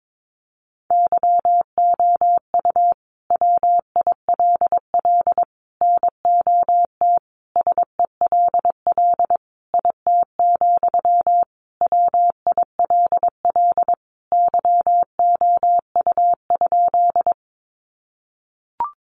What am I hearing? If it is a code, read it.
Morse code, heard as a keyed tone, about 22 wpm, 700 Hz. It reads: YOU WILL NOT SELL IT, WILL YOU?
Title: Sentences from Top 1100 Words 22wpm